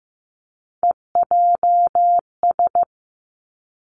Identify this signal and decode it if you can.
Morse code, heard as a keyed tone, about 15 wpm, 700 Hz. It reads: EJS
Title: Three Challenging Letter-Number Combinations 15wpm